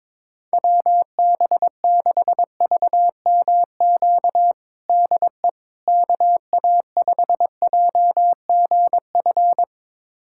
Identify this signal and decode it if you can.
Morse code, heard as a keyed tone, about 22 wpm, 700 Hz. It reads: WB6VMQ DE KA5JGF
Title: USA Callsign DE Callsign - 6 Characters - 22wpm